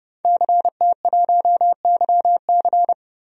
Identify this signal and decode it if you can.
Morse code, heard as a keyed tone, about 30 wpm, 700 Hz. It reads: CT1YC
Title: DXCC-100 Call Signs - 5 Characters - 30wpm